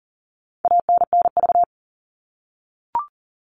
Morse code, heard as a keyed tone, about 40 wpm, 700 Hz. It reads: ANNV
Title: All Cut Numbers - 4-Digit 40wpm